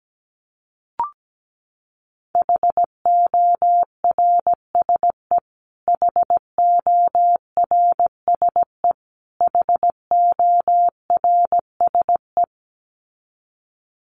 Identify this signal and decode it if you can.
Morse code, heard as a keyed tone, about 17 wpm, 700 Hz. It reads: HORSE HORSE HORSE
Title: Top 500 Words Repeated 17wpm